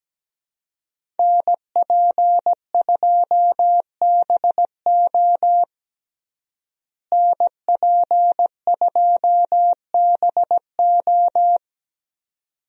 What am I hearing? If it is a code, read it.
Morse code, heard as a keyed tone, about 17 wpm, 700 Hz. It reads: NP2BO NP2BO
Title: USA Call Signs - 5 Characters - Repeated 2x - 17wpm